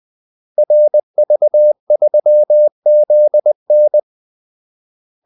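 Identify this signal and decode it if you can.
Morse code, heard as a keyed tone, about 20 wpm, 600 Hz. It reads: RV3ZN
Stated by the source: Morse call-sign practice at 20 wpm.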